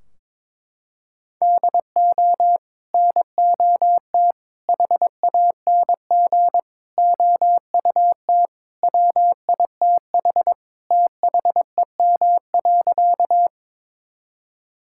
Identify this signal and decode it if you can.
Morse code, heard as a keyed tone, about 22 wpm, 700 Hz. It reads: DO NOT HANG OUT WITH THEM.
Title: Sentences from Top 1600 Words 22wpm